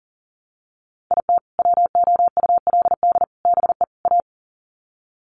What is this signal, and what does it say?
Morse code, heard as a keyed tone, about 38 wpm, 700 Hz. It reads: IT WOULD BE A
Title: Sets of 4 Words 40wpm